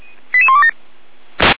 Five-tone selective call (ZVEI), heard as quick runs of five tones, tones of about 70 ms. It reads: 88127